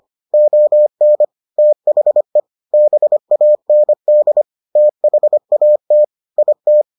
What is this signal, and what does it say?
Morse code, heard as a keyed tone, about 25 wpm, 600 Hz. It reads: ON THE BAND THAT IT